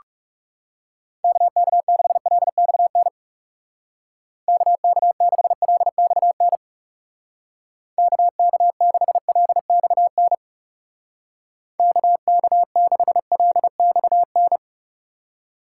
Morse code, heard as a keyed tone, about 36 wpm, 700 Hz. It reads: KK6LXN KK6LXN KK6LXN KK6LXN
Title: Speed Racing - USA 2x3 Call Signs 30wpm